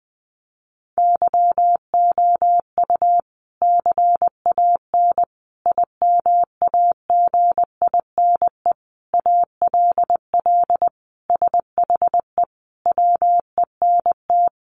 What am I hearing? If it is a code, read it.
Morse code, heard as a keyed tone, about 20 wpm, 700 Hz. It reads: YOU CAN IMAGINE ALL SHE WENT
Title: Sentences from Top 1400 Words 20wpm